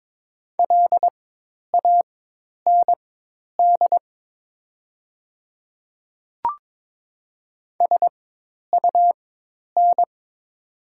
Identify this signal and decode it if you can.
Morse code, heard as a keyed tone, about 22 wpm, spaced as for 10 wpm, 690 Hz. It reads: LAND SUN
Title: 4x Character Spacing - Top 300 Words - 22wpm